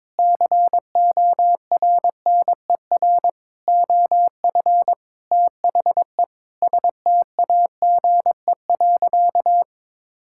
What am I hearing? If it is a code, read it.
Morse code, heard as a keyed tone, about 22 wpm, 700 Hz. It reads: CORNER OF THE STAGE.
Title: Sentences from Top 1600 Words 22wpm